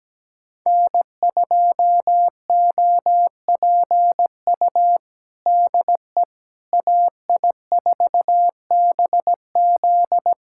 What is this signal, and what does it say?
Morse code, heard as a keyed tone, about 17 wpm, 700 Hz. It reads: N2OPU DE AI4BZ
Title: USA Callsign DE Callsign 17wpm